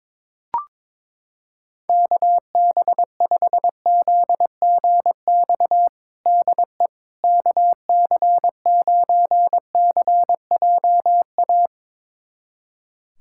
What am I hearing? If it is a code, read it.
Morse code, heard as a keyed tone, about 22 wpm, 700 Hz. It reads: KB5ZGX DE KC9CJA
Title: USA Callsign DE Callsign - 6 Characters - 22wpm